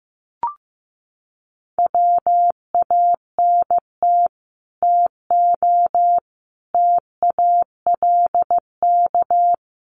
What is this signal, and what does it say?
Morse code, heard as a keyed tone, about 15 wpm, 700 Hz. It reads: WANT TO TALK
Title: Sets of 5 Words Encore 15wpm